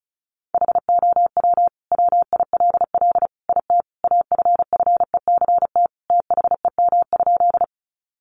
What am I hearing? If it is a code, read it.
Morse code, heard as a keyed tone, about 34 wpm, 700 Hz. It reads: HOW WILL IT AFFECT THEM?
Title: Sentences from Top 900 Words 35wpm